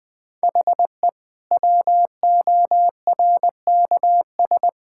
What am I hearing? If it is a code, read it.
Morse code, heard as a keyed tone, about 20 wpm, 700 Hz. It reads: HE WORKS
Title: Sentences with 2-Word Phrases - 20wpm